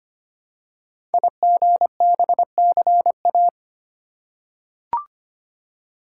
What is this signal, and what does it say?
Morse code, heard as a keyed tone, about 25 wpm, 700 Hz. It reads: IGBCA
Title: Code Groups - 5 Letters - 25wpm